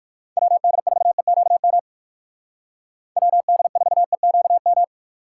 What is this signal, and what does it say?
Morse code, heard as a keyed tone, about 58 wpm, 700 Hz. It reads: WD4EQM WD4EQM